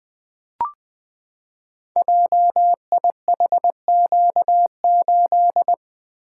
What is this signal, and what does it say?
Morse code, heard as a keyed tone, about 20 wpm, 700 Hz. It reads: JIHQ8
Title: Code Groups - 5 Letter:Numbers - 20wpm